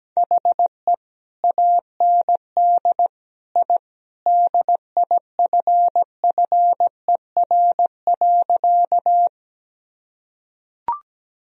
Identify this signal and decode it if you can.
Morse code, heard as a keyed tone, about 17 wpm, 700 Hz. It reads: HE AND I DIFFER.
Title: Sentences from Top 1000 Words - Review Part 1 - 17wpm